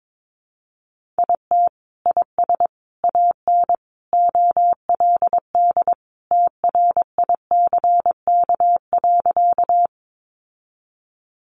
Morse code, heard as a keyed tone, about 22 wpm, 700 Hz. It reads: IT IS AN OLD TRICK.
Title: Sentences from Top 1500 Words 22wpm